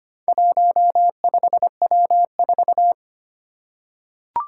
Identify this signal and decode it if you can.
Morse code, heard as a keyed tone, about 25 wpm, 700 Hz. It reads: 15W4
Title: Four Challenging Letter-Number Combinations 25wpm